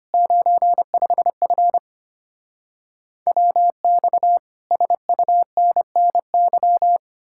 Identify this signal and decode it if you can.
Morse code, heard as a keyed tone, about 33 wpm, 700 Hz. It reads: 95F WX SUNNY